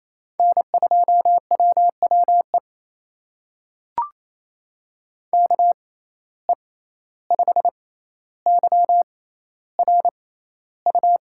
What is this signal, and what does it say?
Morse code, heard as a keyed tone, about 28 wpm, spaced as for 14 wpm, 700 Hz. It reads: N2WWE KE5YRU
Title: USA Call Signs - 6x Character Spacing - 28wpm